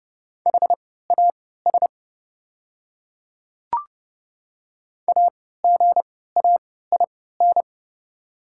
Morse code, heard as a keyed tone, about 30 wpm, spaced as for 17 wpm, 690 Hz. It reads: HAS AGAIN
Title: 3x Character Spacing - Top 200 Words - 30wpm